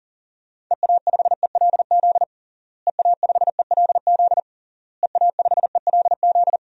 Morse code, heard as a keyed tone, about 40 wpm, 700 Hz. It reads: EA5ELZ EA5ELZ EA5ELZ
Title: DXCC-100 Call Signs - 6 Characters - Repeated 3x - 40wpm